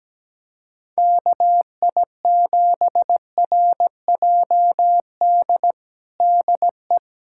Morse code, heard as a keyed tone, about 17 wpm, 700 Hz. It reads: KI7RJD DE
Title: USA Callsign DE Callsign - 6 Characters - 17wpm